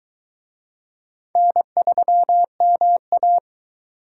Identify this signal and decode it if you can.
Morse code, heard as a keyed tone, about 23 wpm, 700 Hz. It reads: N3MA